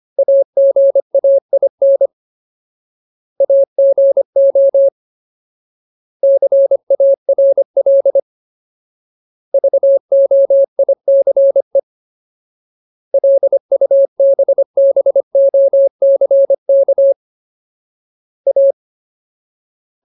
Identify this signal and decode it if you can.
Morse code, heard as a keyed tone, about 25 wpm, 550 Hz. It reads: AGAIN AGO CARL VOICE LUBBOCK A
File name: Ultimate Word List / 25WPM Ultimate